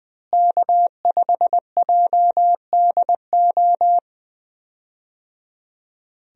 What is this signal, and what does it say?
Morse code, heard as a keyed tone, about 21 wpm, 700 Hz. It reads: K5JDO
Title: USA Call Signs - 5 Characters - Repeated 3x - 20wpm